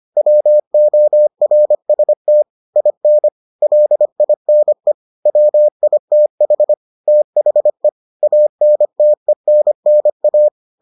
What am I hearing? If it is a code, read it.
Morse code, heard as a keyed tone, about 25 wpm, 600 Hz. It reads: WORST IN LINE WITH THE ANTENNA